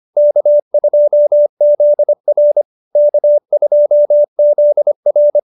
Morse code, heard as a keyed tone, about 25 wpm, 590 Hz. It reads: K2ZR K2ZR
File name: QSO302 25